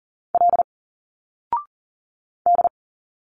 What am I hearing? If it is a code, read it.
Morse code, heard as a keyed tone, about 39 wpm, 700 Hz. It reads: L D
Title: Single Letters 40wpm